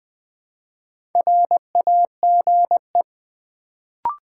Morse code, heard as a keyed tone, about 20 wpm, 700 Hz. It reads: RAGE
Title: Build A Word - 2 Letters At A Time - 4-Letter Words - 20wpm